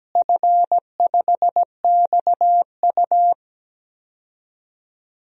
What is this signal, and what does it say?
Morse code, heard as a keyed tone, about 17 wpm, 700 Hz. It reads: F5XU
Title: DXCC-100 Call Signs - 4 Characters - Repeated 2x - 17wpm